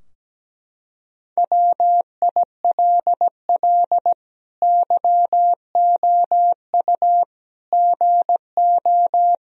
Morse code, heard as a keyed tone, about 17 wpm, 700 Hz. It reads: WILL YOU GO
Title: Sentences from Top 1000 Words - Review Part 2 - 17wpm